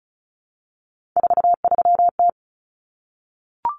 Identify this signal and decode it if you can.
Morse code, heard as a keyed tone, about 34 wpm, 700 Hz. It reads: 43T
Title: Common Cut Numbers - 3-Digit 35wpm